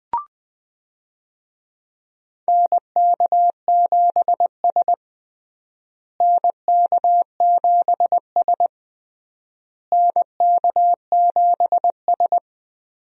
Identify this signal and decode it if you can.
Morse code, heard as a keyed tone, about 21 wpm, 700 Hz. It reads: NK7S NK7S NK7S
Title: USA Call Signs - 4 Characters - Repeated 3x - 20wpm